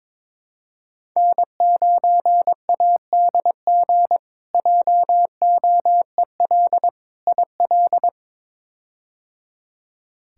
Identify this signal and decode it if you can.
Morse code, heard as a keyed tone, about 21 wpm, 700 Hz. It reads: N9ADG JOEL IL
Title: K1USN SST QSO Practice - 22wpm